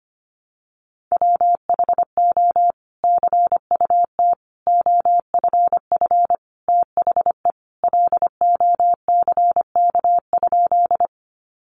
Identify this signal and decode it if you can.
Morse code, heard as a keyed tone, about 25 wpm, 700 Hz. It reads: WHO CUT OFF THE LOCK?
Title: Sentences from Top 1300 Words 25wpm